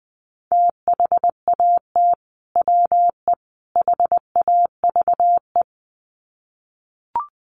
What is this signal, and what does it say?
Morse code, heard as a keyed tone, about 20 wpm, 700 Hz. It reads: THAT WE HAVE
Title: Sets of 3 Words 20wpm